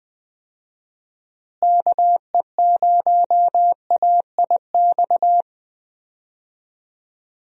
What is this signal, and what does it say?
Morse code, heard as a keyed tone, about 20 wpm, 700 Hz. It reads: KE0AIX
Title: USA Call Signs - 6 Characters - Repeated 2x - 20wpm